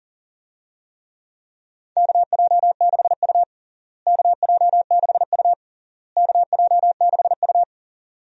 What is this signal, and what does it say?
Morse code, heard as a keyed tone, about 40 wpm, 700 Hz. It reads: KJ6U KJ6U KJ6U
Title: USA Call Signs - 4 Characters - Repeated 3x - 40wpm